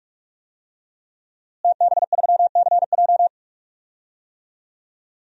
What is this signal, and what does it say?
Morse code, heard as a keyed tone, about 46 wpm, 700 Hz. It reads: TB3CJ
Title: DXCC-100 Call Signs - 5 Characters - Repeated 2x - 45wpm